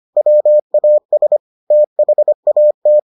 Morse code, heard as a keyed tone, about 25 wpm, 600 Hz. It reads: WAS THAT